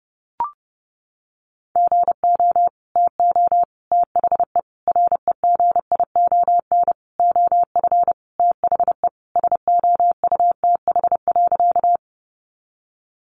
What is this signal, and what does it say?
Morse code, heard as a keyed tone, about 30 wpm, 700 Hz. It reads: GO TO THE REGION OF THE SOUTH.
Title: Sentences from Top 1500 Words 30wpm